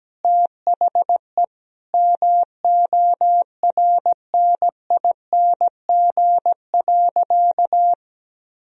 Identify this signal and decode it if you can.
Morse code, heard as a keyed tone, about 17 wpm, 700 Hz. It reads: THE MORNING.
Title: Sentences from Top 1000 Words - Review Part 2 - 17wpm